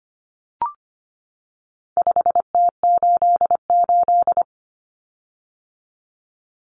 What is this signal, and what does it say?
Morse code, heard as a keyed tone, about 25 wpm, 700 Hz. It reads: 5T88
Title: Common Cut Numbers - 4-Digit 25wpm